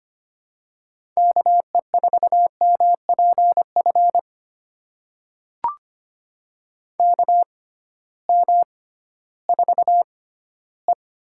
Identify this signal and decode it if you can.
Morse code, heard as a keyed tone, about 25 wpm, spaced as for 15 wpm, 700 Hz. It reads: KE4MPF KM4E
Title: USA Call Signs - 6x Character Spacing - 25wpm